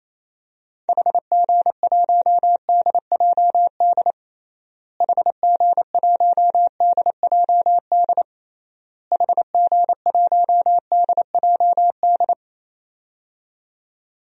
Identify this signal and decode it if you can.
Morse code, heard as a keyed tone, about 28 wpm, 700 Hz. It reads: HG1DJD HG1DJD HG1DJD
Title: DXCC-100 Call Signs - 6 Characters - Repeated 3x - 28wpm